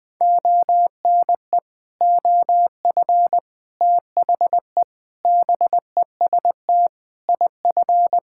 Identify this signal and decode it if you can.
Morse code, heard as a keyed tone, about 20 wpm, 700 Hz. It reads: ONE OF THE BEST IF
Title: Sentences with 8-Word Phrases - Spoken Beginning - 20wpm